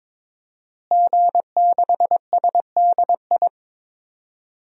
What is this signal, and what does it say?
Morse code, heard as a keyed tone, about 22 wpm, 700 Hz. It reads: G6SDI
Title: DXCC-100 Call Signs - 5 Characters - Repeated 2x - 22wpm